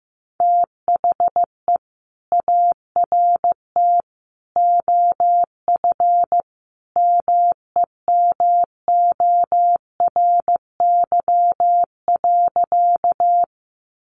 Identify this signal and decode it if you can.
Morse code, heard as a keyed tone, about 15 wpm, 700 Hz. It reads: THE ART OF MEMORY.
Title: Sentences from Top 1300 Words 15wpm